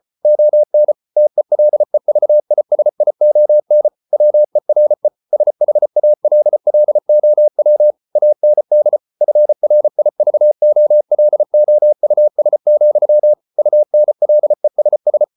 Morse code, heard as a keyed tone, about 34 wpm, 600 Hz. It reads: ON TELEVISION WERE SHALLOW AND FRIVOLOUS, UNLESS